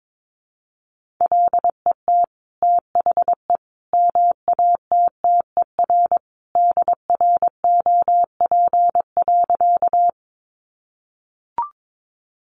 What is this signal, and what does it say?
Morse code, heard as a keyed tone, about 22 wpm, 700 Hz. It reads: LET THE MATTER DROP.
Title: Sentences from Top 1400 Words 22wpm